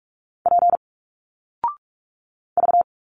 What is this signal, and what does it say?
Morse code, heard as a keyed tone, about 46 wpm, 700 Hz. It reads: P V
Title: Single Letters 45wpm